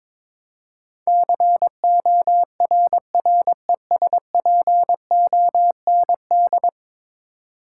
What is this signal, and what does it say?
Morse code, heard as a keyed tone, about 22 wpm, 700 Hz. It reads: CORRESPOND
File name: practice / 22wpm / 10 Letter Words - First 3 Letters Given - 22wpm